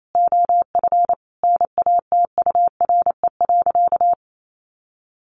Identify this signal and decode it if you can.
Morse code, heard as a keyed tone, about 28 wpm, 700 Hz. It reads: OF NATURE.
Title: Sentences from Top 1200 Words 28wpm